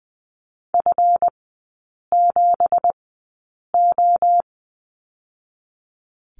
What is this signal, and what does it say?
Morse code, heard as a keyed tone, about 20 wpm, spaced as for 8 wpm, 690 Hz. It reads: F7O
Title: 166 Lesson 27e - Number 7 - Three Characters 20wpm